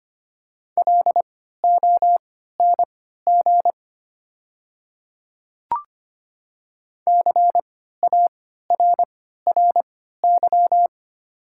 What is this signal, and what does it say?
Morse code, heard as a keyed tone, about 25 wpm, spaced as for 14 wpm, 700 Hz. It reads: LONG CARRY
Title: 3x Character Spacing - Top 300 Words - 25wpm